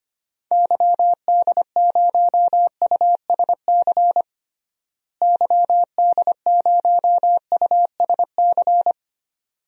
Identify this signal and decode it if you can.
Morse code, heard as a keyed tone, about 25 wpm, 700 Hz. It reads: YD0USC YD0USC
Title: DXCC-100 Call Signs - 6 Characters - Repeated 2x - 25wpm